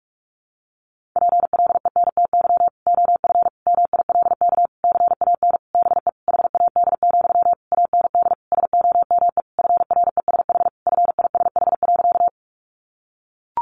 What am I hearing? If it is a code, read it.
Morse code, heard as a keyed tone, about 44 wpm, 700 Hz. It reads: PLENTY OF MILK CAN BE HAD, AND SOME FRESH FISH.
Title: Sentences from Top 1500 Words 45wpm